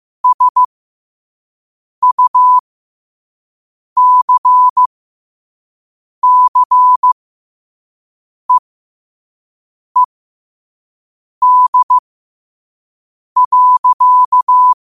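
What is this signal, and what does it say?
Morse code, heard as a keyed tone, about 15 wpm, spaced as for 5 wpm, 1000 Hz. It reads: SUCCEED.
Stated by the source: Morse practice at 5 wpm.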